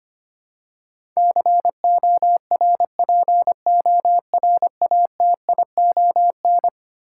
Morse code, heard as a keyed tone, about 25 wpm, 700 Hz. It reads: CORPORATION
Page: morse-code-ninja-practice-25wpm